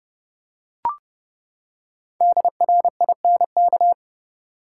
Morse code, heard as a keyed tone, about 32 wpm, 700 Hz. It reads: DRINK
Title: Five Letter Words - 30wpm